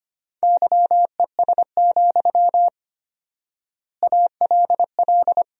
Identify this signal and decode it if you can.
Morse code, heard as a keyed tone, about 25 wpm, 700 Hz. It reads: YES, ALL
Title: 4x Word-Spacing - Sentences from Top 500 Words - 25wpm